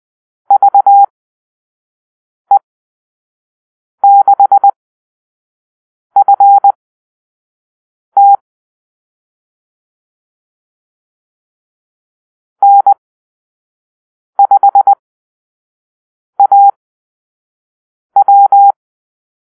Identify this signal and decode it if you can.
Morse code, heard as a keyed tone, about 20 wpm, spaced as for 5 wpm, 800 Hz. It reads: VE6FT N5AW
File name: arrl / Call-Sign Drill-4